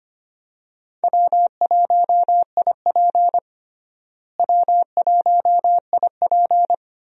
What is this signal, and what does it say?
Morse code, heard as a keyed tone, about 25 wpm, 700 Hz. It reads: W1IP W1IP
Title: USA Call Signs - 4 Characters - Repeated 2x - 25wpm